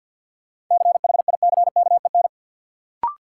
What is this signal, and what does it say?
Morse code, heard as a keyed tone, about 52 wpm, 700 Hz. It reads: CHICKEN
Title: Sentences from Top 1000 Words - Review Part 5 - New Word Drill - 50wpm